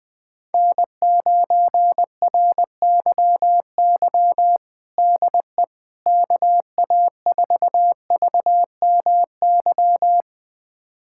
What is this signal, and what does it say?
Morse code, heard as a keyed tone, about 20 wpm, 700 Hz. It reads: N9RYY DE KA4VMY
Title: USA Callsign DE Callsign 20wpm